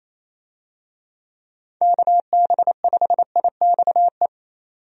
Morse code, heard as a keyed tone, about 28 wpm, 700 Hz. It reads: KB5IXE
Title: USA Call Signs - 6 Characters - 28wpm